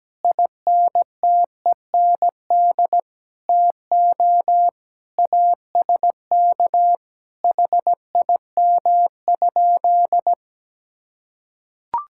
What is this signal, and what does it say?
Morse code, heard as a keyed tone, about 17 wpm, 700 Hz. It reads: INTEND TO ASK HIM?
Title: Sentences from Top 1600 Words 17wpm